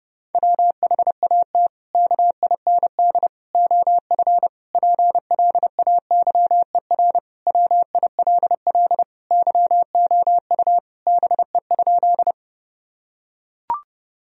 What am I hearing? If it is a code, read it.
Morse code, heard as a keyed tone, about 30 wpm, 700 Hz. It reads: WHAT KIND OF PLAYER WILL YOU BE?
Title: Sentences from Top 1000 Words - Review Part 4 - 30wpm